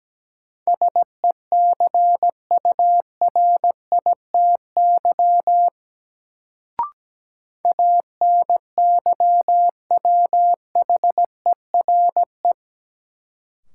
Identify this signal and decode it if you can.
Morse code, heard as a keyed tone, about 17 wpm, 700 Hz. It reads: SECURITY ANYWHERE
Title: Sentences from Top 1400 Words - New Word Drill Alt - 17wpm